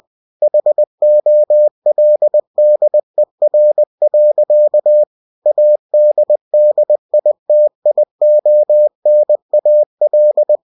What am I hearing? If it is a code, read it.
Morse code, heard as a keyed tone, about 20 wpm, 600 Hz. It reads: HOLDER. ADDITIONAL
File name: cw-books / winnie-the-pooh / winnie-20wpm-page-0115